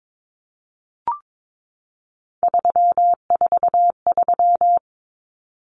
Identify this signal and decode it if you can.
Morse code, heard as a keyed tone, about 22 wpm, 700 Hz. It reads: 343
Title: Common Cut Numbers - 3-Digit 22wpm